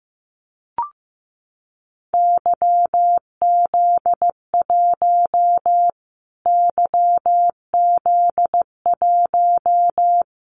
Morse code, heard as a keyed tone, about 15 wpm, 700 Hz. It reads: YZ1 YZ1
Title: Callsign Prefixes for Top 25 DXCC - Repeated 3x - 15wpm